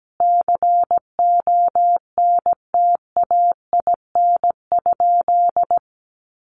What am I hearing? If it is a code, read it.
Morse code, heard as a keyed tone, about 17 wpm, 700 Hz. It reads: CONTAIN?
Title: Sentences from Top 500 Words 17wpm